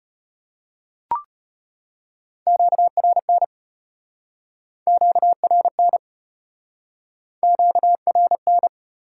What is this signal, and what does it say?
Morse code, heard as a keyed tone, about 33 wpm, 700 Hz. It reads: QRN QRN QRN